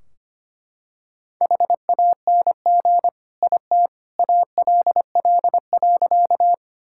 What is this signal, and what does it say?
Morse code, heard as a keyed tone, about 25 wpm, 700 Hz. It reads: HANG IT ALL.
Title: Sentences from Top 1600 Words 25wpm